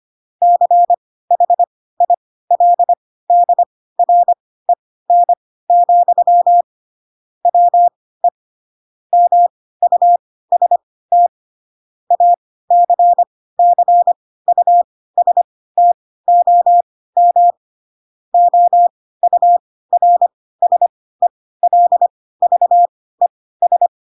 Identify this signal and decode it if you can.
Morse code, heard as a keyed tone, about 25 wpm, spaced as for 16 wpm, 700 Hz. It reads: CHILDREN, WE MUST ACCUSTOM OURSELVES
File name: Pinocchio / Pinocchio ch7 10 25 ik1hge